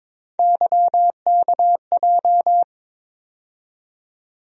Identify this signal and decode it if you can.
Morse code, heard as a keyed tone, about 21 wpm, 700 Hz. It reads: YKJ
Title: USA Call Sign Postfixes - 3 Letters 22wpm